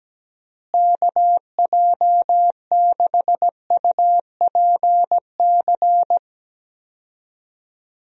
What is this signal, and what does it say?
Morse code, heard as a keyed tone, about 17 wpm, 700 Hz. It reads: KJ6UPC
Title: USA Call Signs - 6 Characters - Repeated 2x - 17wpm